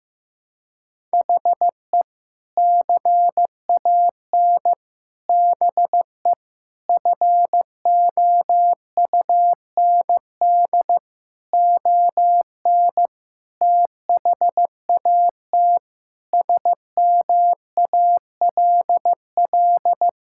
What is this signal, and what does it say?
Morse code, heard as a keyed tone, about 15 wpm, 700 Hz. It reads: HE CAN BE FOUND ON THAT SMALL
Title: Sentences from Top 1600 Words 15wpm